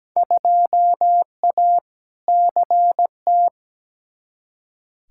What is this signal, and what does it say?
Morse code, heard as a keyed tone, about 17 wpm, 700 Hz. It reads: 2A CT
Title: Field Day Exchanges Encore - 17wpm